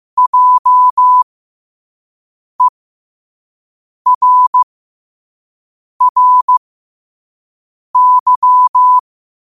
Morse code, heard as a keyed tone, about 15 wpm, spaced as for 5 wpm, 1000 Hz. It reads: JERRY
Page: New quotes every day in morse code at 5 Words per minute.